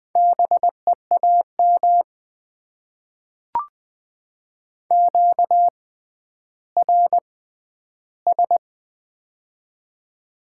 Morse code, heard as a keyed tone, about 20 wpm, spaced as for 11 wpm, 700 Hz. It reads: BEAM QRS
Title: Top 90 QSO Elements - 6x Character Spacing - 20wpm